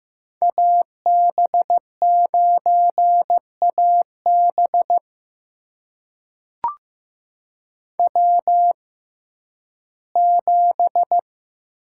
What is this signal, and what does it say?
Morse code, heard as a keyed tone, about 15 wpm, spaced as for 11 wpm, 700 Hz. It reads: AB9AB W7
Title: USA Call Signs - 5 Characters - 6x Character Spacing - 15wpm